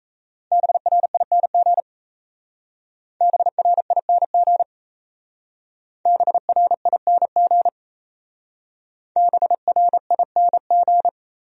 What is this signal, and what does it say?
Morse code, heard as a keyed tone, about 34 wpm, 700 Hz. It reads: BRING BRING BRING BRING
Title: Speed-Racing Top 500 Words 28wpm